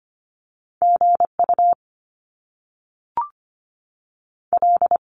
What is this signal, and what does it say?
Morse code, heard as a keyed tone, about 25 wpm, spaced as for 3 wpm, 700 Hz. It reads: G UL
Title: Top Two Letter Combinations in Words 25wpm